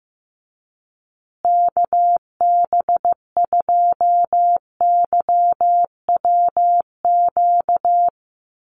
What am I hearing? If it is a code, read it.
Morse code, heard as a keyed tone, about 15 wpm, 700 Hz. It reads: KB2YWQ
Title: USA Call Signs Repeated 2x 15wpm